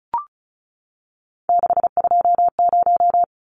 Morse code, heard as a keyed tone, about 34 wpm, 700 Hz. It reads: 620
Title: Numbers - 3-Digit 35wpm